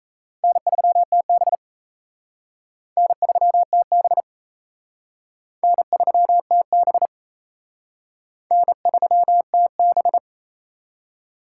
Morse code, heard as a keyed tone, about 34 wpm, 700 Hz. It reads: N3TB N3TB N3TB N3TB
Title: Speed Racing - USA 1x2 Call Signs 28wpm